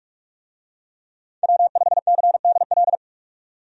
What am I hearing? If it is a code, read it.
Morse code, heard as a keyed tone, about 41 wpm, 700 Hz. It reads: W5CDL
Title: USA Call Signs - 5 Characters - 45wpm